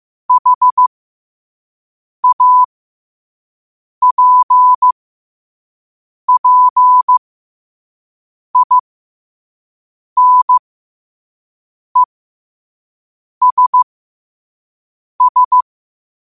Morse code, heard as a keyed tone, about 15 wpm, spaced as for 5 wpm, 1000 Hz. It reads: HAPPINESS
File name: qotdfiles / 5wpm